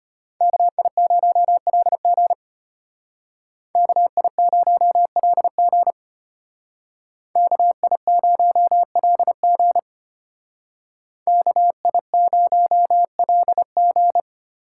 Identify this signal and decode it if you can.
Morse code, heard as a keyed tone, about 32 wpm, 700 Hz. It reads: KI0LG KI0LG KI0LG KI0LG